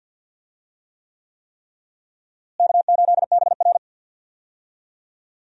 Morse code, heard as a keyed tone, about 50 wpm, 700 Hz. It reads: K8BR
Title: USA Call Signs - 4 Characters - Repeated 3x - 50wpm